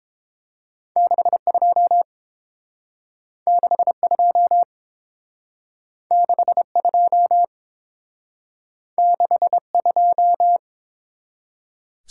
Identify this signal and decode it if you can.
Morse code, heard as a keyed tone, about 30 wpm, 700 Hz. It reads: 62 62 62 62